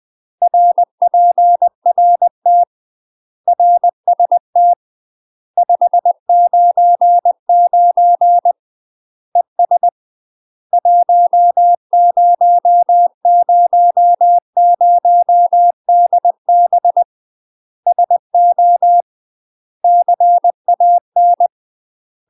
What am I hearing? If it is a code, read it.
Morse code, heard as a keyed tone, about 20 wpm, 700 Hz. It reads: RPRT RST 599 ES 1000DB SO CAN
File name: UK MASTER QSOs TRAINING / 20WPM / UK MASTER QSOs 0028